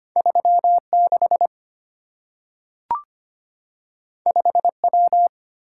Morse code, heard as a keyed tone, about 25 wpm, 700 Hz. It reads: 36 5W
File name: Two Letter-Number Combinations - 25wpm